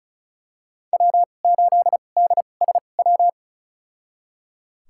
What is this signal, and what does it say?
Morse code, heard as a keyed tone, about 36 wpm, spaced as for 25 wpm, 700 Hz. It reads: W8DSW
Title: USA Call Signs - 5 Characters - 2x Character Spacing - 35wpm